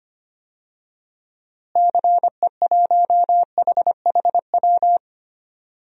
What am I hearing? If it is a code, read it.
Morse code, heard as a keyed tone, about 25 wpm, 700 Hz. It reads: CE1HHW
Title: DXCC-100 Call Signs - 6 Characters - Repeated 3x - 25wpm